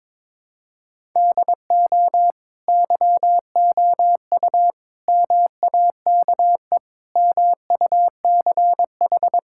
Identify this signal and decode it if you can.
Morse code, heard as a keyed tone, about 21 wpm, 700 Hz. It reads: DO YOU MAKE MUCH
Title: Sentences from Top 1000 Words - Review Part 6 - 22wpm